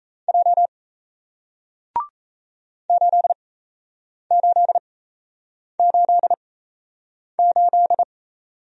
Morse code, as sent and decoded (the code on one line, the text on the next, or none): .--- / ---.. ---.. ---.. ---..
J 8888